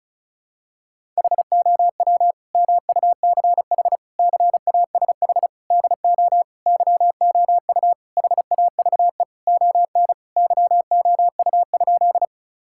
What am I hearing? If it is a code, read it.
Morse code, heard as a keyed tone, about 34 wpm, 700 Hz. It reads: HOW MUCH CASH DO YOU HAVE ON YOU?
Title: Sentences from Top 1000 Words - Review Part 5 - 35wpm